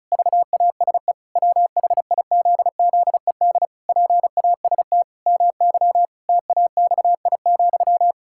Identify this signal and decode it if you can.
Morse code, heard as a keyed tone, about 34 wpm, 700 Hz. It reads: VASE WHIZZED PAST MY TAXI,
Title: Pangram Challenge v2 - 35wpm